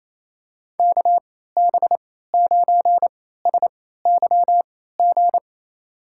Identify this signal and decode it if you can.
Morse code, heard as a keyed tone, about 28 wpm, spaced as for 16 wpm, 700 Hz. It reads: KB9SYG
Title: USA Call Signs - 6 Characters - 3x Character Spacing - 28wpm